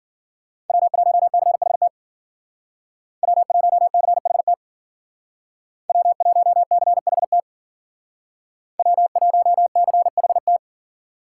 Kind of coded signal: Morse code

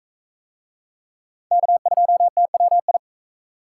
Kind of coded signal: Morse code

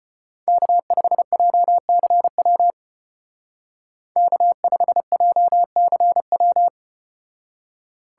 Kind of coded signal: Morse code

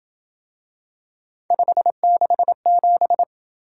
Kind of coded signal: Morse code

